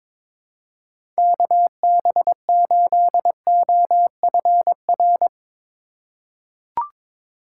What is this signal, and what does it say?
Morse code, heard as a keyed tone, about 22 wpm, 700 Hz. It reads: KB8OFR
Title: USA Call Signs - 6 Characters - 3x Character Spacing - 22wpm